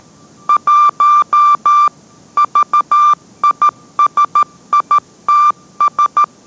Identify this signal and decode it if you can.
Morse code, heard as a keyed tone, about 13 wpm, 1180 Hz. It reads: 1 VISITS